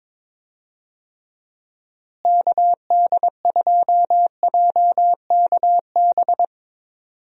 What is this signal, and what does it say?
Morse code, heard as a keyed tone, about 22 wpm, 700 Hz. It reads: KD2JKB
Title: USA Call Signs - 6 Characters - Repeated 2x - 22wpm